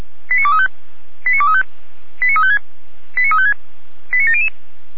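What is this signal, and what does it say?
Five-tone selective call (ZVEI), heard as quick runs of five tones, tones of about 70 ms. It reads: 89235 89235 89356 89356 89700
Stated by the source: ZVEI five-tone selective call received by SDR, from the fire brigade Feuerwehr Orlamünde.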